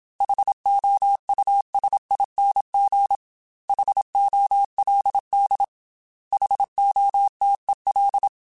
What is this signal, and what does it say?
Morse code, heard as a keyed tone, about 27 wpm, 800 Hz. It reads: HOUSING HOLD HOTEL
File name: H Runs / H Runs mp3s / H Runs @25